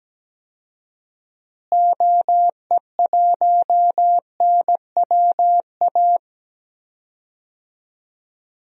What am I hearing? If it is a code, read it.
Morse code, heard as a keyed tone, about 17 wpm, 700 Hz. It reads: OE1NWA
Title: DXCC-100 Call Signs - 6 Characters - Repeated 2x - 17wpm